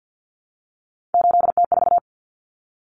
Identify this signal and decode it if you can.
Morse code, heard as a keyed tone, about 50 wpm, 700 Hz. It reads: 8T4
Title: Common Cut Numbers - 3-Digit 50wpm